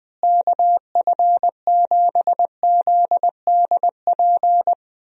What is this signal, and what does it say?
Morse code, heard as a keyed tone, about 21 wpm, 700 Hz. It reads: KF7ZDP
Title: USA Call Signs - 6 Characters - Repeated 3x - 20wpm